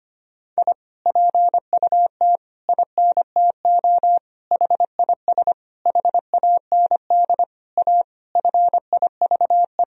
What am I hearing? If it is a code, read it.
Morse code, heard as a keyed tone, about 25 wpm, 700 Hz. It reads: I PUT INTO HIS HAND A FIVE
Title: Sentences from Top 1000 Words - Review Part 2 - 25wpm